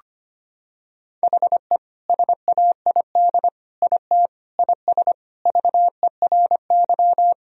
Morse code, heard as a keyed tone, about 25 wpm, 700 Hz. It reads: HE SAID IT IS VERY
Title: Sentences from Top 1000 Words - Review Part 2 - 25wpm